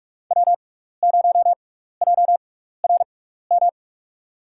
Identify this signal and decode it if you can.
Morse code, heard as a keyed tone, about 41 wpm, spaced as for 15 wpm, 700 Hz. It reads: W0JRM
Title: USA Call Signs - 5 Characters - 6x Character Spacing - 45wpm